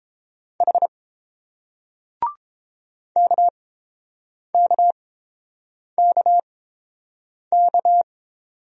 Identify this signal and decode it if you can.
Morse code, heard as a keyed tone, about 28 wpm, spaced as for 7 wpm, 690 Hz. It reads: H KKKK